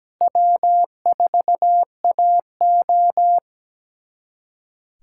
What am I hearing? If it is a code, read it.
Morse code, heard as a keyed tone, about 17 wpm, 700 Hz. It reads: W4AO
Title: USA Callsign DE Callsign - 4 Characters - 17wpm